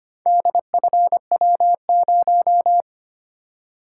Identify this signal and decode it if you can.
Morse code, heard as a keyed tone, about 26 wpm, 700 Hz. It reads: DFW0
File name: practice / 25wpm / Four Letter-Number Combinations - 25wpm